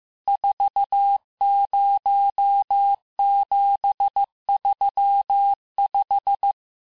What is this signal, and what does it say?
Morse code, heard as a keyed tone, about 15 wpm, 800 Hz. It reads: 40735
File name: cours-du-lundi files / cours-1585-texte-chiffre-vitesse-14mpm